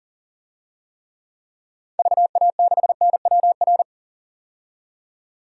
Morse code, heard as a keyed tone, about 40 wpm, 700 Hz. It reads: VA6NWR
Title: DXCC-100 Call Signs - 6 Characters - 40wpm